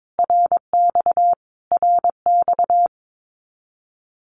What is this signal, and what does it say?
Morse code, heard as a keyed tone, about 22 wpm, 700 Hz. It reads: RX RX
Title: Abbreviations Repeated 3x - 22wpm